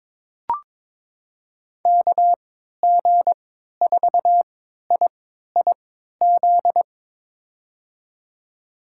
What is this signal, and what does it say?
Morse code, heard as a keyed tone, about 22 wpm, spaced as for 13 wpm, 700 Hz. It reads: KG4IIZ
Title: USA Call Signs - 6 Characters - 3x Character Spacing - 22wpm